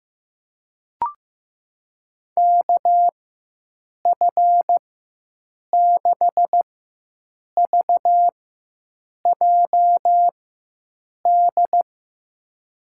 Morse code, heard as a keyed tone, about 16 wpm, spaced as for 7 wpm, 700 Hz. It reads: KF6VJD
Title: USA Call Signs - 6 Characters - 4x Character Spacing - 15wpm